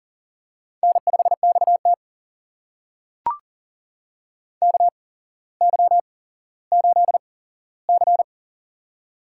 Morse code, heard as a keyed tone, about 40 wpm, spaced as for 17 wpm, 700 Hz. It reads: N5XT KY8C
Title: USA Call Signs - 4 Characters - 8x Character Spacing - 40wpm